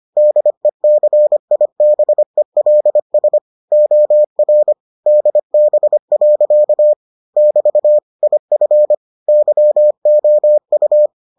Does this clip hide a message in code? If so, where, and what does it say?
Yes: Morse code from 0.2 to 11.1 s, heard as a keyed tone, about 25 wpm, 600 Hz. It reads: DECIBELS OR DB. <BT> IF YOU